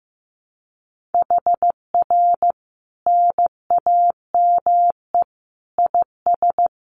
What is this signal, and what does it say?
Morse code, heard as a keyed tone, about 15 wpm, 700 Hz. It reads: HR NAME IS
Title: Common QSO Phrases 15wpm